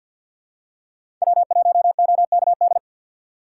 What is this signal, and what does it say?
Morse code, heard as a keyed tone, about 47 wpm, 700 Hz. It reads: W1OKD
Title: USA Call Signs - 5 Characters - 4x Character Spacing - 50wpm